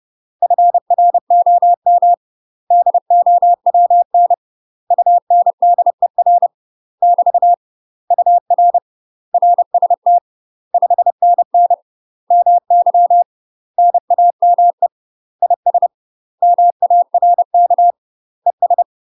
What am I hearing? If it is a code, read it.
Morse code, heard as a keyed tone, about 30 wpm, 700 Hz. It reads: FROM DOWN UNDER <BT> UR RST 5NN MY NAME IS MARK ES
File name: UK MASTER QSOs TRAINING / 30WPM / UK MASTER QSOs 0046